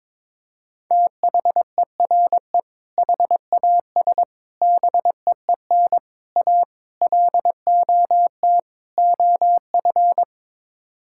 Morse code, heard as a keyed tone, about 22 wpm, 700 Hz. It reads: THERE HAS BEEN A LOT OF
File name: practice / 22wpm / Sets of 6 Words - 22wpm